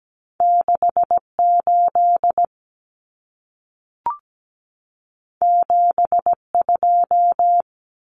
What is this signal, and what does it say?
Morse code, heard as a keyed tone, about 17 wpm, 700 Hz. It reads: 68 72
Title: Common Cut Numbers - 2-Digit 17wpm